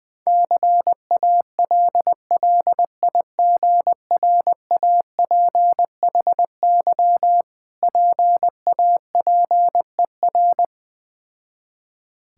Morse code, heard as a keyed tone, about 20 wpm, 690 Hz. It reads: CALLIGRAPHY PAPER
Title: Difficult Words with Hints - 20wpm